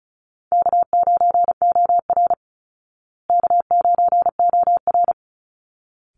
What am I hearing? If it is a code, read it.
Morse code, heard as a keyed tone, about 34 wpm, 700 Hz. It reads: K9OR K9OR
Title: Easy Call Signs Repeated 2x - 35wpm